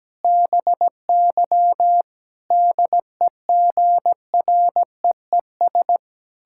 Morse code, heard as a keyed tone, about 17 wpm, 700 Hz. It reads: BY DEGREES
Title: Pangram Challenge v2 - 17wpm